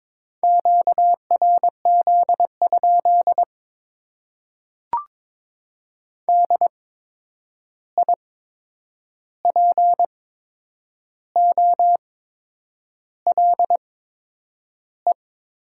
Morse code, heard as a keyed tone, about 22 wpm, spaced as for 8 wpm, 700 Hz. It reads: QRZ? DIPOLE
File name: Top 90 QSO Elements - 8x Character Spacing - 22wpm